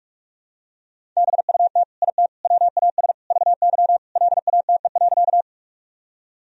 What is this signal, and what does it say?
Morse code, heard as a keyed tone, about 45 wpm, 700 Hz. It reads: BUT IT WAS VY LATE.
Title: Sentences from Top 300 Words and Abbreviations 45wpm